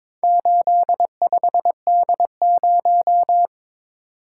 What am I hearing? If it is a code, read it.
Morse code, heard as a keyed tone, about 22 wpm, 700 Hz. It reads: 85D0
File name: practice / 22wpm / Four Letter-Number Combinations - 22wpm